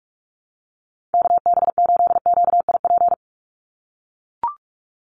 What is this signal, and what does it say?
Morse code, heard as a keyed tone, about 44 wpm, 700 Hz. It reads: KB8QIP
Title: USA Call Signs 45wpm